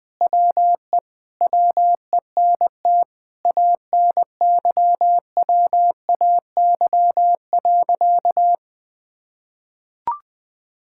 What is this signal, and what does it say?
Morse code, heard as a keyed tone, about 20 wpm, 700 Hz. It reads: WE WENT ANYWAY.
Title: Sentences with 4-Word Phrases - 20wpm